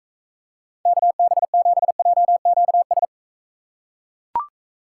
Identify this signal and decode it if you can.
Morse code, heard as a keyed tone, about 44 wpm, 700 Hz. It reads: KB7JQS